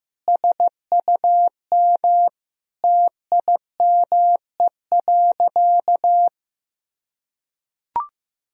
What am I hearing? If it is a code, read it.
Morse code, heard as a keyed tone, about 15 wpm, 700 Hz. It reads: SUM TIME.
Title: Sentences from Top 300 Words and Abbreviations 15wpm